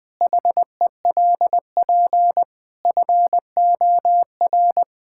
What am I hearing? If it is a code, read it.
Morse code, heard as a keyed tone, about 20 wpm, 700 Hz. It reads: HELP FOR
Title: Pangram Challenge v2 - 20wpm